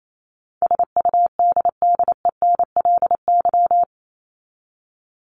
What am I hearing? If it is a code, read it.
Morse code, heard as a keyed tone, about 28 wpm, 700 Hz. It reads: SUDDENLY
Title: LY Suffix - Top 100 Words with Base Words 28wpm